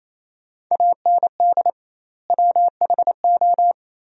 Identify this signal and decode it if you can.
Morse code, heard as a keyed tone, about 28 wpm, 700 Hz. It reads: AND WHO
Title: 2x Word-Spacing - Sentences from Top 200 Words - 28wpm